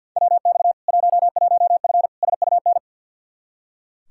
Morse code, heard as a keyed tone, about 45 wpm, 700 Hz. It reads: WX 11F SUN
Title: Practice QSO - Exchange 2 - Info - 50wpm